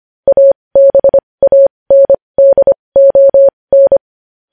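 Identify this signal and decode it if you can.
Morse code, heard as a keyed tone, about 25 wpm, spaced as for 20 wpm, 550 Hz. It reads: ABANDON
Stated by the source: Morse code practice of news headlines at 20 wpm.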